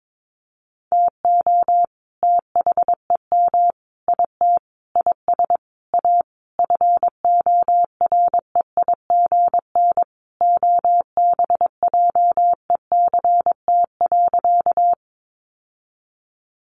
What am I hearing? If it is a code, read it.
Morse code, heard as a keyed tone, about 22 wpm, 700 Hz. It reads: TO THEM IT IS A FOREIGN OBJECT.
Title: Sentences from Top 1400 Words 22wpm